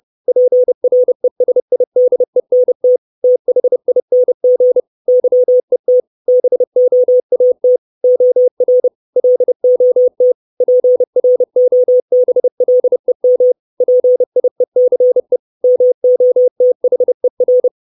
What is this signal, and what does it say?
Morse code, heard as a keyed tone, about 30 wpm, 500 Hz. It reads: PRESIDENT THING YET BOAT OR LOT PROBLEM PIECE MOTHER
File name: 632 Words 30WPM